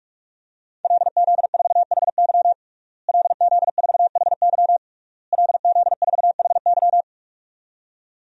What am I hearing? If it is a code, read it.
Morse code, heard as a keyed tone, about 43 wpm, 700 Hz. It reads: LZ4HY LZ4HY LZ4HY
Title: DXCC-100 Call Signs - 5 Characters - Repeated 3x - 45wpm